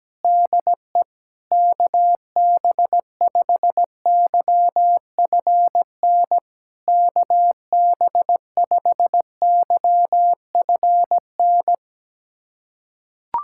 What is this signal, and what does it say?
Morse code, heard as a keyed tone, about 18 wpm, 700 Hz. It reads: DE KB5YFN KB5YFN
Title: SOTA QSO Practice 17wpm